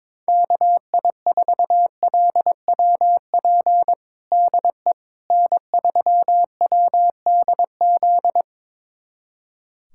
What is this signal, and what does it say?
Morse code, heard as a keyed tone, about 22 wpm, 700 Hz. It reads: KI4LWP DE N3WDZ
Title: USA Callsign DE Callsign 22wpm